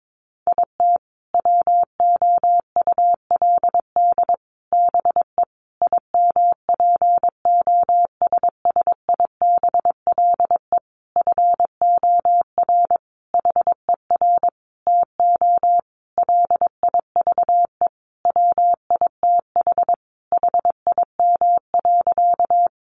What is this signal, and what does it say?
Morse code, heard as a keyed tone, about 22 wpm, 700 Hz. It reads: IT WOULD BE IMPOSSIBLE FOR HER TO LIVE WITH HIM.
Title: Sentences from Top 1100 Words 22wpm